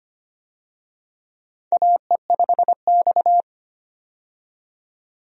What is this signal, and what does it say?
Morse code, heard as a keyed tone, about 25 wpm, 700 Hz. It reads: AE5X
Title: USA Call Signs - 4 Characters - 25wpm